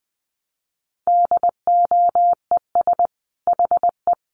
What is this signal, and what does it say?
Morse code, heard as a keyed tone, about 20 wpm, 700 Hz. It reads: DOES HE
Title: Sentences from Top 1300 Words 20wpm